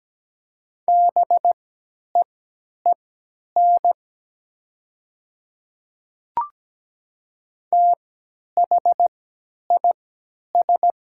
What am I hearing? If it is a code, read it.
Morse code, heard as a keyed tone, about 17 wpm, spaced as for 10 wpm, 700 Hz. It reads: BEEN THIS
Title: 3x Character Spacing - Top 100 Words - 17wpm